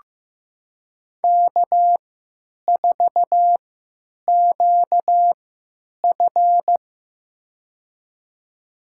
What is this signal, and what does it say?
Morse code, heard as a keyed tone, about 15 wpm, spaced as for 8 wpm, 700 Hz. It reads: K4QF
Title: USA Call Signs - 3x Character Spacing - 15wpm